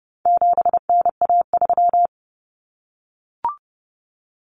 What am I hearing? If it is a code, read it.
Morse code, heard as a keyed tone, about 30 wpm, 700 Hz. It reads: 7NA3
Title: Common Cut Numbers - 4-Digit 30wpm